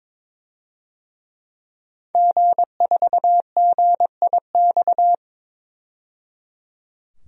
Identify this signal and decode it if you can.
Morse code, heard as a keyed tone, about 22 wpm, 700 Hz. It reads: G4GIX
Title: DXCC-100 Call Signs - 5 Characters - 22wpm